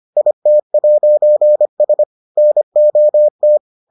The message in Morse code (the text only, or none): IT'S NOT